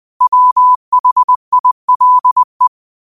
Morse code, heard as a keyed tone, about 20 wpm, 1000 Hz. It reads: WHILE
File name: qotdfiles / 20wpm